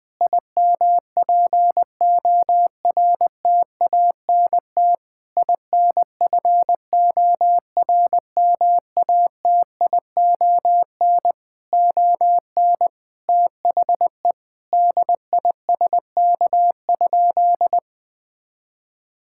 Morse code, heard as a keyed tone, about 21 wpm, 700 Hz. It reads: IMPORTANT INFORMATION ON THE DISK?
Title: Sentences from Top 1000 Words - Review Part 6 - 20wpm